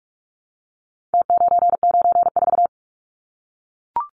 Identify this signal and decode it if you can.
Morse code, heard as a keyed tone, about 44 wpm, 700 Hz. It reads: T994
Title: Numbers - Contest Style - 4-Digit 45wpm